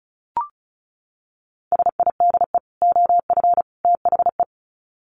Morse code, heard as a keyed tone, about 34 wpm, 700 Hz. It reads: SIDE OF THE
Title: Sets of 3 Words 35wpm